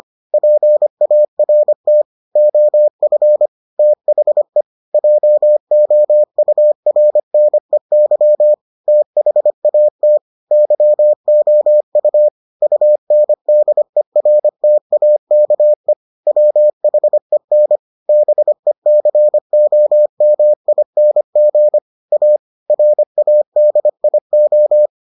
Morse code, heard as a keyed tone, about 25 wpm, 600 Hz. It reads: PART OF THE JOURNEY THAT YOU UNDERTAKE WHEN BECOMING A RADIO